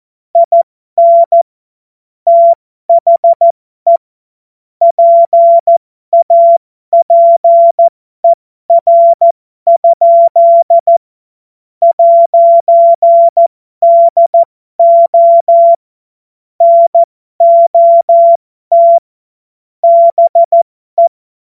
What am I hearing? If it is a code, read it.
Morse code, heard as a keyed tone, about 14 wpm, spaced as for 12 wpm, 680 Hz. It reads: IN THE PAPER? 'DO NOT BE
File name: Morsekurs / Witze